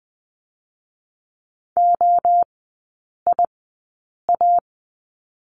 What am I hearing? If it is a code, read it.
Morse code, heard as a keyed tone, about 20 wpm, 700 Hz. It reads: O I A
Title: 029 Lesson 6e - Letter i - Three Characters 20wpm